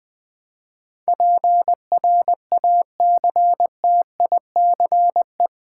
Morse code, heard as a keyed tone, about 20 wpm, 700 Hz. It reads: PRACTICE
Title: Eight Letter Words - 20wpm